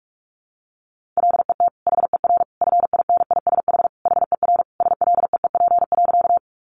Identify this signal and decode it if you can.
Morse code, heard as a keyed tone, about 44 wpm, 700 Hz. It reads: LET HER FINISH HER SLEEP.
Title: Sentences from Top 1200 Words 45wpm